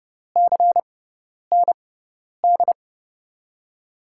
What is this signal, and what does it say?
Morse code, heard as a keyed tone, about 29 wpm, spaced as for 10 wpm, 700 Hz. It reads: CND
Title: USA Call Signs - 6 Characters - 6x Character Spacing - 30wpm